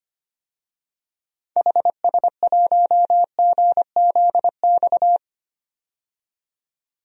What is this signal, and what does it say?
Morse code, heard as a keyed tone, about 25 wpm, 700 Hz. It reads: HS1GZX
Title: DXCC-100 Call Signs - 6 Characters - 25wpm